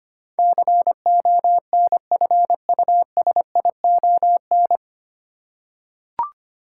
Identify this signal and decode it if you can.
Morse code, heard as a keyed tone, about 25 wpm, 700 Hz. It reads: CONFUSION
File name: practice / 25wpm / 9 Letter Words - First 3 Letters Given - 25wpm